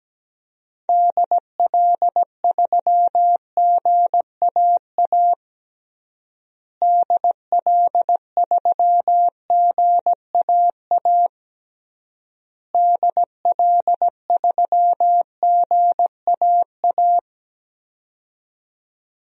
Morse code, heard as a keyed tone, about 17 wpm, 700 Hz. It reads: DL3GAA DL3GAA DL3GAA
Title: DXCC-100 Call Signs - 6 Characters - Repeated 3x - 17wpm